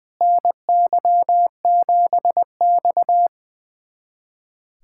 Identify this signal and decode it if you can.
Morse code, heard as a keyed tone, about 20 wpm, 700 Hz. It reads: NY7X
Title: USA Callsign DE Callsign - 4 Characters - 20wpm